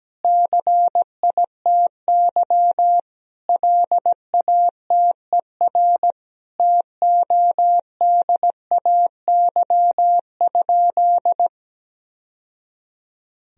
Morse code, heard as a keyed tone, about 17 wpm, 700 Hz. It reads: CITY LATER TODAY?
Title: Sentences from Top 1000 Words - Review Part 2 - 17wpm